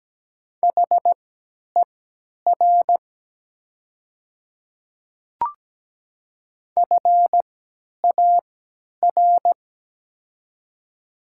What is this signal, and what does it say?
Morse code, heard as a keyed tone, about 17 wpm, spaced as for 10 wpm, 700 Hz. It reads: HER FAR
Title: 3x Character Spacing - Top 500 Words - 17wpm